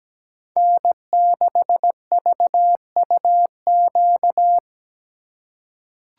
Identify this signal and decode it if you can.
Morse code, heard as a keyed tone, about 17 wpm, 700 Hz. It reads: N6VUQ
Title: USA Call Signs - 2x Character Spacing - 17wpm